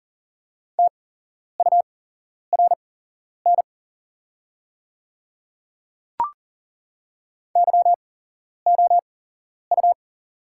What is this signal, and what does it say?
Morse code, heard as a keyed tone, about 40 wpm, spaced as for 11 wpm, 690 Hz. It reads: TURN YOU
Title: 8x Character Spacing - Top 300 Words - 40wpm